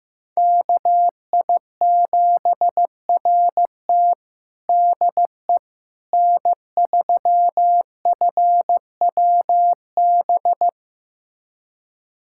Morse code, heard as a keyed tone, about 16 wpm, 700 Hz. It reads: KI7RT DE N3FWB
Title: USA Callsign DE Callsign - 5 Characters - 15wpm